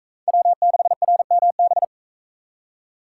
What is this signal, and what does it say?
Morse code, heard as a keyed tone, about 42 wpm, 700 Hz. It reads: W6RMB